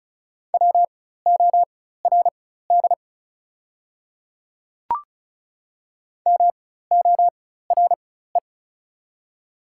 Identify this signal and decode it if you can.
Morse code, heard as a keyed tone, about 36 wpm, spaced as for 16 wpm, 690 Hz. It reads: WORD MORE
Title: 4x Character Spacing - Top 100 Words - 35wpm